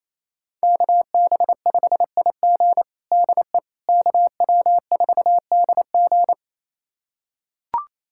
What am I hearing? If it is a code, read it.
Morse code, heard as a keyed tone, about 27 wpm, 700 Hz. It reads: KB5IG DE KW4DG
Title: USA Callsign DE Callsign - 5 Characters - 28wpm